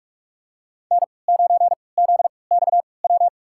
Morse code, heard as a keyed tone, about 41 wpm, spaced as for 25 wpm, 700 Hz. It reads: N9ZXW
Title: USA Call Signs - 3x Character Spacing - 45wpm